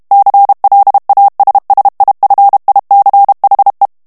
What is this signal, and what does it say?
Morse code, heard as a keyed tone, about 32 wpm, 800 Hz. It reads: CLASSIFICHE
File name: ITALIEN texte clair 2 vitesse 30 mots minute